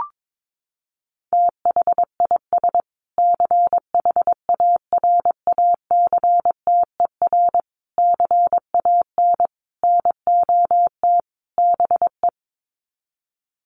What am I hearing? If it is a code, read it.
Morse code, heard as a keyed tone, about 22 wpm, 700 Hz. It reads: THIS CHARACTER CAN NOT BE
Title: Sets of 5 Words Encore2 22wpm